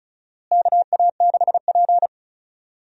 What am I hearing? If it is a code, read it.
Morse code, heard as a keyed tone, about 36 wpm, 700 Hz. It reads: KA6P
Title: USA Call Signs - 4 Characters - Repeated 2x - 35wpm